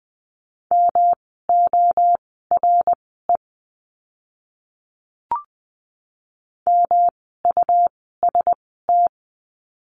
Morse code, heard as a keyed tone, about 20 wpm, spaced as for 14 wpm, 700 Hz. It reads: MORE MUST
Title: 237 Lesson 38f - Letter q - Words 20wpm